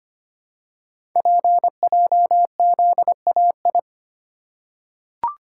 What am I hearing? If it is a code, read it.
Morse code, heard as a keyed tone, about 25 wpm, 700 Hz. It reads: PJZAI
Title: Code Groups - 5 Letters - 25wpm